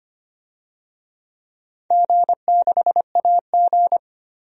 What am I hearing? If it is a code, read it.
Morse code, heard as a keyed tone, about 25 wpm, 700 Hz. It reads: G6AG
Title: DXCC-100 Call Signs - 4 Characters - 25wpm